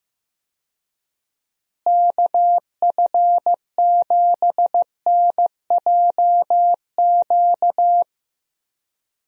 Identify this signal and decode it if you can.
Morse code, heard as a keyed tone, about 15 wpm, 700 Hz. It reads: KF7NJQ
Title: USA Call Signs - 6 Characters - Repeated 3x - 15wpm